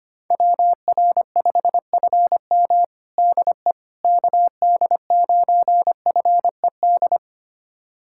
Morse code, heard as a keyed tone, about 25 wpm, 700 Hz. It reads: WR5FM DE KD9FED
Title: USA Callsign DE Callsign 25wpm